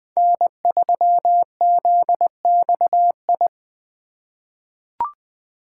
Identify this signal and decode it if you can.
Morse code, heard as a keyed tone, about 20 wpm, 700 Hz. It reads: N3ZXI
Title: USA Call Signs - 5 Characters - 3x Character Spacing - 20wpm